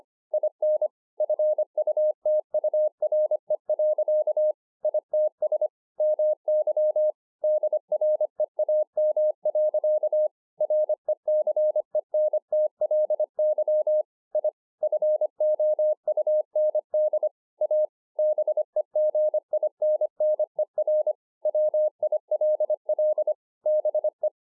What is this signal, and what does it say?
Morse code, heard as a keyed tone, about 26 wpm, 610 Hz. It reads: IN FUTURE. ITS MY DREAM. RECENTLY I FOUND A BEGINNER WILL BE
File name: 2011-04-mp3 / BEGINNERS DIARY 25WPM